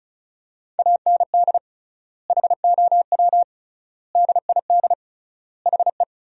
Morse code, heard as a keyed tone, about 34 wpm, 700 Hz. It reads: AND HOW DID HE
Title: 3x Word-Spacing - Sentences from Top 400 Words - 35wpm